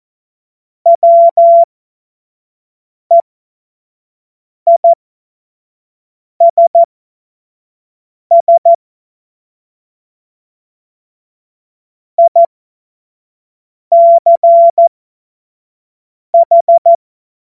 Morse code, heard as a keyed tone, about 14 wpm, spaced as for 5 wpm, 680 Hz. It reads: WEISS ICH
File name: Morsekurs / Witze / Witz 14. März 2023 25 F70